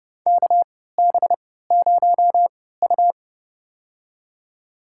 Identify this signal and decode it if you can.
Morse code, heard as a keyed tone, about 30 wpm, spaced as for 17 wpm, 700 Hz. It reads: KB0U
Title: USA Call Signs - 4 Characters - 3x Character Spacing - 30wpm